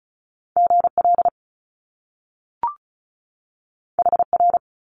Morse code, heard as a keyed tone, about 36 wpm, 700 Hz. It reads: GL HR
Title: Top Two Letter Combinations in Words 35wpm